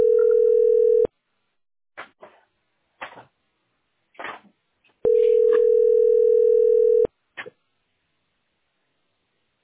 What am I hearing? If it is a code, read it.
Ringback tone (North America), 440 + 480 Hz.
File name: 86e6e8c2-daea-419c-a684